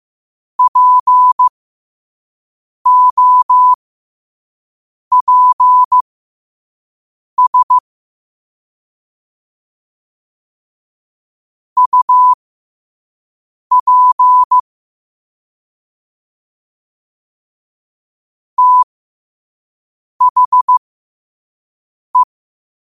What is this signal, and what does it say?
Morse code, heard as a keyed tone, about 15 wpm, spaced as for 5 wpm, 1000 Hz. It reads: POPS UP THE
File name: qotdfiles / 5wpm